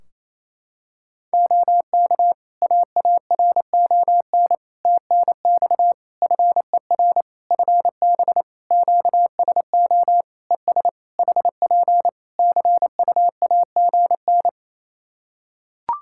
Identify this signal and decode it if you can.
Morse code, heard as a keyed tone, about 27 wpm, 700 Hz. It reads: OK AARON TNX FER FB QSO ES HP CUAGN
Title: K1USN SST QSO Practice 28wpm